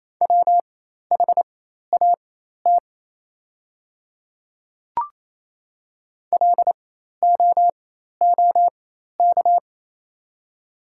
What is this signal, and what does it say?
Morse code, heard as a keyed tone, about 28 wpm, spaced as for 13 wpm, 700 Hz. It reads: WHAT LOOK
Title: 4x Character Spacing - Top 200 Words - 28wpm